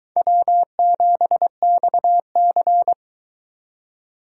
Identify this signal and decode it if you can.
Morse code, heard as a keyed tone, about 22 wpm, 700 Hz. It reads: W7XC